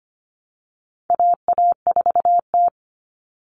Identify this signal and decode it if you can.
Morse code, heard as a keyed tone, about 25 wpm, 700 Hz. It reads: AA4T
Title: All Cut Numbers - 4-Digit 25wpm